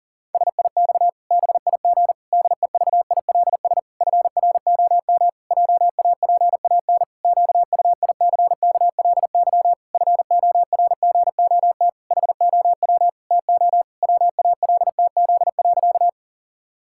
Morse code, heard as a keyed tone, about 40 wpm, 700 Hz. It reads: SIX BIG DEVILS FROM JAPAN QUICKLY FORGOT HOW TO WALTZ.
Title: Pangram Challenge v2 - 40wpm